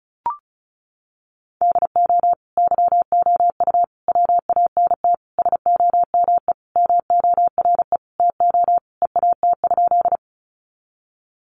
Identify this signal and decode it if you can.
Morse code, heard as a keyed tone, about 34 wpm, 700 Hz. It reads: DO YOU WANT SOME MORE TO EAT?
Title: Sentences from Top 300 Words 35wpm